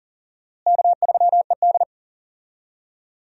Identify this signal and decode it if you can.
Morse code, heard as a keyed tone, about 40 wpm, 700 Hz. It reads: K3ED
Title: USA Call Signs - 4 Characters - Repeated 3x - 40wpm